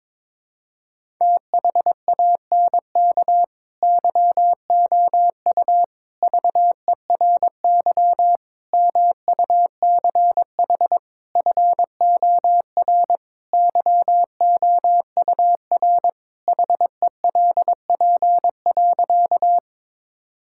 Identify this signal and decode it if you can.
Morse code, heard as a keyed tone, about 22 wpm, 700 Hz. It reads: THANK YOU VERY MUCH FOR YOUR HELP.
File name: practice / 22wpm / Sentences with 4-Word Phrases - 22wpm